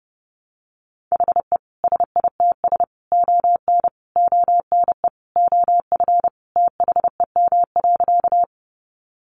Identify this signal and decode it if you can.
Morse code, heard as a keyed tone, about 30 wpm, 700 Hz. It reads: HE SITS ON ONE OF THEM.
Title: Additional Easy Phrases 30wpm